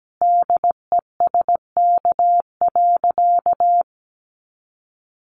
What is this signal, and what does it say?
Morse code, heard as a keyed tone, about 17 wpm, 700 Hz. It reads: DESK.
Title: Additional Easy Phrases 17wpm